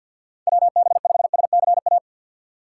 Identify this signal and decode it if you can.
Morse code, heard as a keyed tone, about 50 wpm, 700 Hz. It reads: WB5SCA
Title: USA Call Signs - 6 Characters - 4x Character Spacing - 50wpm